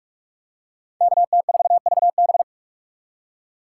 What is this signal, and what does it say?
Morse code, heard as a keyed tone, about 46 wpm, 700 Hz. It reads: KT4VB
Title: USA Call Signs - 5 Characters - 6x Character Spacing - 45wpm